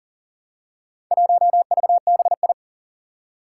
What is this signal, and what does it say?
Morse code, heard as a keyed tone, about 40 wpm, 700 Hz. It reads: 1VBI
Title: Four Challenging Letter-Number Combinations 40wpm